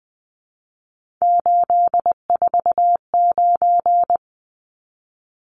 Morse code, heard as a keyed tone, about 20 wpm, 700 Hz. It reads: 849
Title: Numbers - 3-Digit 20wpm